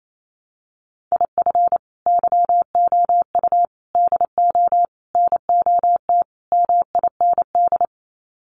Morse code, heard as a keyed tone, about 28 wpm, 700 Hz. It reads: IF YOU DO NOT MIND
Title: Sets of 5 Words 28wpm